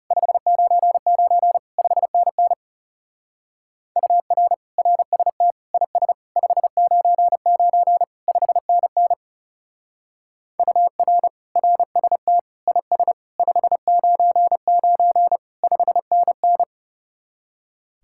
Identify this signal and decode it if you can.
Morse code, heard as a keyed tone, about 35 wpm, 700 Hz. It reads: 599 5NN UR RST IS 599 5NN UR RST IS 599 5NN
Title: Speed-Racing Common QSO Phrases - 30wpm